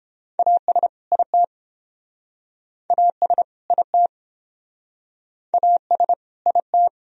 Morse code, heard as a keyed tone, about 31 wpm, 700 Hz. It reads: AS IT AS IT AS IT